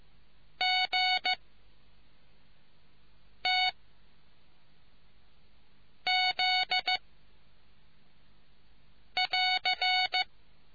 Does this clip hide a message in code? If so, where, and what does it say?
Yes: Morse code from 0.6 to 10.2 s, heard as a keyed tone, about 15 wpm, 740 Hz. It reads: G T Z <AR>